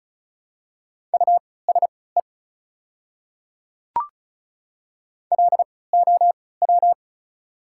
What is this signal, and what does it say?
Morse code, heard as a keyed tone, about 36 wpm, spaced as for 20 wpm, 700 Hz. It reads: USE LOW
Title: 3x Character Spacing - Top 300 Words - 35wpm